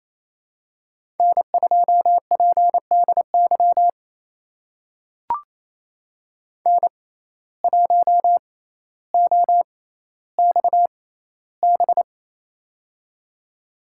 Morse code, heard as a keyed tone, about 27 wpm, spaced as for 14 wpm, 700 Hz. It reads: N2PDY N1OXB
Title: USA Call Signs - 5 Characters - 6x Character Spacing - 28wpm